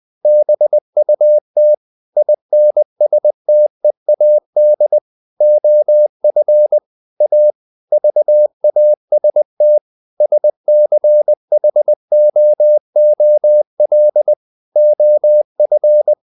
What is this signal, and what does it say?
Morse code, heard as a keyed tone, about 20 wpm, 600 Hz. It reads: BUT INSTEAD OF A VAST SCHOOL OF
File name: the old man and the sea part 2 20wpm